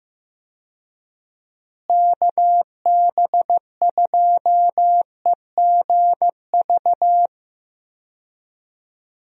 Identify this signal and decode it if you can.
Morse code, heard as a keyed tone, about 16 wpm, 700 Hz. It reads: KB2EGV
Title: USA Call Signs - 6 Characters - Repeated 3x - 15wpm